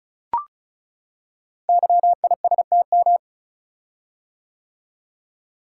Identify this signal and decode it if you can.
Morse code, heard as a keyed tone, about 36 wpm, 700 Hz. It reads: YISTM
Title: Code Groups - 5 Letters - 35wpm